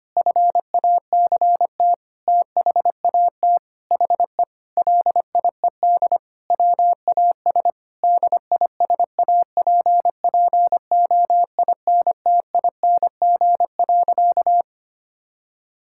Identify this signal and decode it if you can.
Morse code, heard as a keyed tone, about 25 wpm, 700 Hz. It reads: FACT THAT HE LIED WAS DISAPPOINTING.
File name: practice / 25wpm / Sentences with 4-Word Phrases - 25wpm